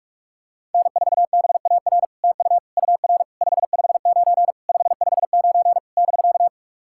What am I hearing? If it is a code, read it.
Morse code, heard as a keyed tone, about 46 wpm, 700 Hz. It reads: N4BAR TU UR 559 559 <BK>
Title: SOTA QSO Practice 45wpm